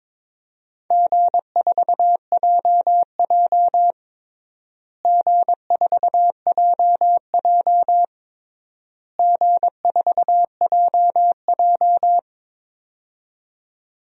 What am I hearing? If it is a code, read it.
Morse code, heard as a keyed tone, about 22 wpm, 700 Hz. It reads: G4JJ G4JJ G4JJ
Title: DXCC-100 Call Signs - 4 Characters - Repeated 3x - 22wpm